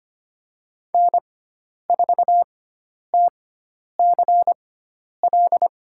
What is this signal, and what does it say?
Morse code, heard as a keyed tone, about 25 wpm, spaced as for 10 wpm, 700 Hz. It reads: N4TCL
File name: Lesson 16g - Letter c - Call Signs 25wpm